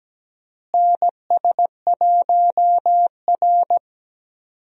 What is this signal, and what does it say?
Morse code, heard as a keyed tone, about 17 wpm, 700 Hz. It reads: NS1R
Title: USA Call Signs - 4 Characters - Repeated 3x - 17wpm